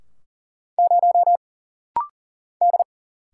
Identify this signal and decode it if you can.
Morse code, heard as a keyed tone, about 40 wpm, 700 Hz. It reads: 0 D
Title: Single Letter-Number Alt - Rapid-Fire 40wpm